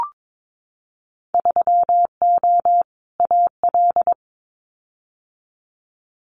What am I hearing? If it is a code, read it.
Morse code, heard as a keyed tone, about 22 wpm, 700 Hz. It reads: 3O AL
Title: Winter Field Day Exchanges 22wpm